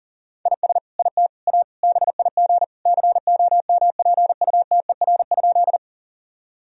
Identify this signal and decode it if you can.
Morse code, heard as a keyed tone, about 40 wpm, 700 Hz. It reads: IS IT A BIG COMPUTER?
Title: Sentences from Top 1000 Words - Review Part 3 - 40wpm